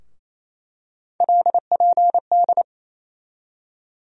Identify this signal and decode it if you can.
Morse code, heard as a keyed tone, about 28 wpm, 700 Hz. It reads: LPD
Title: USA Call Sign Postfixes - 3 Letters 28wpm